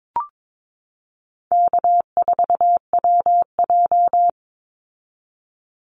Morse code, heard as a keyed tone, about 22 wpm, 700 Hz. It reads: K4WJ
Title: Easy Call Signs Repeated 3x - 22wpm